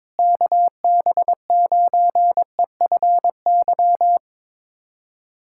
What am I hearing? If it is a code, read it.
Morse code, heard as a keyed tone, about 22 wpm, 700 Hz. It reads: KB9EFY
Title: USA Callsign DE Callsign 22wpm